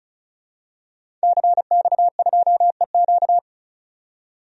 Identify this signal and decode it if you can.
Morse code, heard as a keyed tone, about 36 wpm, 700 Hz. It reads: CX2EQ
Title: DXCC-100 Call Signs - 5 Characters - 35wpm